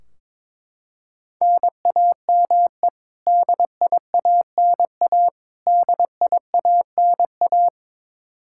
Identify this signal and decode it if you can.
Morse code, heard as a keyed tone, about 22 wpm, 700 Hz. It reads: NAME DIANA DIANA
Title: Practice QSO - Exchange 1 - Beginning - 22wpm